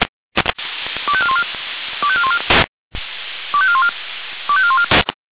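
Five-tone selective call (ZVEI), heard as quick runs of five tones, tones of about 70 ms. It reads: 25414 25414 25414 25414
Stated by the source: ZVEI five-tone selective call 25420 received with an SDR.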